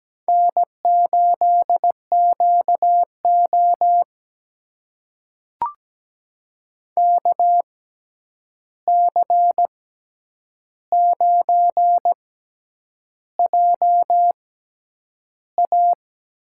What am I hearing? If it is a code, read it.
Morse code, heard as a keyed tone, about 17 wpm, spaced as for 8 wpm, 700 Hz. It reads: N8QO KC9JA
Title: USA Call Signs - 6x Character Spacing - 17wpm